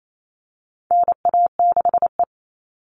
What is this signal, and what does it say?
Morse code, heard as a keyed tone, about 28 wpm, 700 Hz. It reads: NA6E
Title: All Cut Numbers - 4-Digit 28wpm